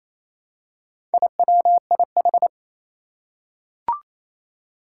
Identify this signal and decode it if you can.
Morse code, heard as a keyed tone, about 27 wpm, 700 Hz. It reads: IWIH
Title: Four Challenging Letter-Number Combinations 28wpm